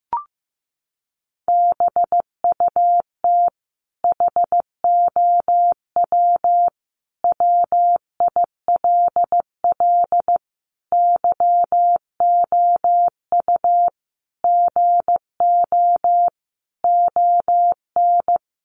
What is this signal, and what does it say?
Morse code, heard as a keyed tone, about 15 wpm, 700 Hz. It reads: BUT HOW WILL YOU GO ON
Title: Sentences from Top 1100 Words 15wpm